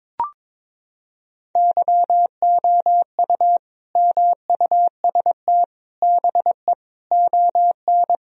Morse code, heard as a keyed tone, about 22 wpm, 700 Hz. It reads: YOU MUST BE ON
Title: Sentences from Top 1000 Words - Review Part 4 - 22wpm